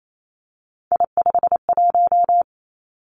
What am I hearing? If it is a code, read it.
Morse code, heard as a keyed tone, about 28 wpm, 700 Hz. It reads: I51
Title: Three Challenging Letter-Number Combinations 28wpm